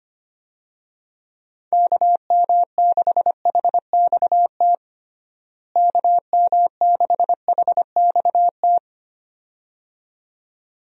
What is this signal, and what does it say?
Morse code, heard as a keyed tone, about 25 wpm, 700 Hz. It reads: KM6HXT KM6HXT
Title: USA Call Signs - 6 Characters - Repeated 2x - 25wpm